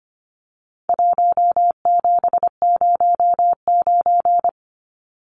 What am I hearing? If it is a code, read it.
Morse code, heard as a keyed tone, about 25 wpm, 700 Hz. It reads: 1709
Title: Numbers - Contest Style - 4-Digit 25wpm